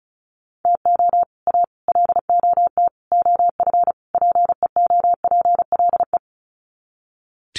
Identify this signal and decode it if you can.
Morse code, heard as a keyed tone, about 34 wpm, 700 Hz. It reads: TO A LOT OF PEOPLE
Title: Sets of 5 Words Encore 35wpm